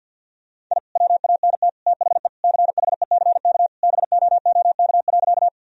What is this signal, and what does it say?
Morse code, heard as a keyed tone, about 50 wpm, 700 Hz. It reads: I WANT THE CHECK BOOK.
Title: Sentences from Top 1000 Words - Review Part 3 - 50wpm